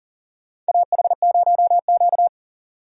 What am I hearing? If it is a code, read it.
Morse code, heard as a keyed tone, about 40 wpm, 700 Hz. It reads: AH0Q
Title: USA Call Signs - 4 Characters - 40wpm